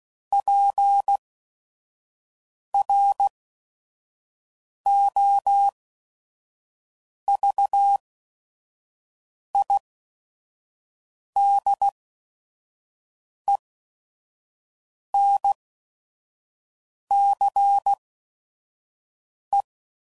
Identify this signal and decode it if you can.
Morse code, heard as a keyed tone, about 16 wpm, spaced as for 5 wpm, 800 Hz. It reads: PROVIDENCE
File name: CW Code Groups @5WPM / fromSM's / Cities